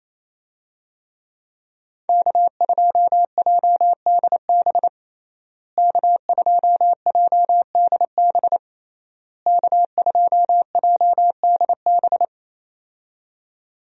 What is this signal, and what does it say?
Morse code, heard as a keyed tone, about 28 wpm, 700 Hz. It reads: K2JDB K2JDB K2JDB
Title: USA Call Signs - 5 Characters - Repeated 3x - 28wpm